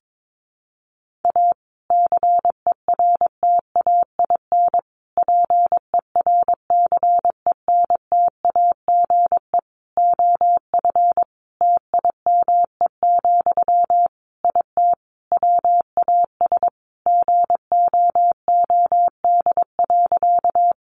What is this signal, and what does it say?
Morse code, heard as a keyed tone, about 22 wpm, 700 Hz. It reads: A CERTAIN PERCENTAGE OF TIME, IT WAS GOOD.
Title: Sentences from Top 1300 Words 22wpm